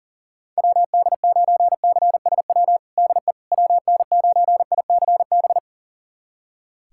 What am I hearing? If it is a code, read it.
Morse code, heard as a keyed tone, about 37 wpm, 700 Hz. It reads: WD9CSW DE WN9ICB
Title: USA Callsign DE Callsign 40wpm